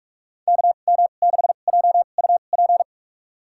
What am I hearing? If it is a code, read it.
Morse code, heard as a keyed tone, about 41 wpm, spaced as for 31 wpm, 700 Hz. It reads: KM6JUP
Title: USA Call Signs - 6 Characters - 2x Character Spacing - 45wpm